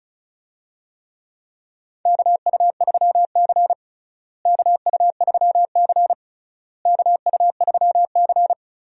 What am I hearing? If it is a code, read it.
Morse code, heard as a keyed tone, about 36 wpm, 700 Hz. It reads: KU3C KU3C KU3C
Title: USA Call Signs - 4 Characters - Repeated 3x - 35wpm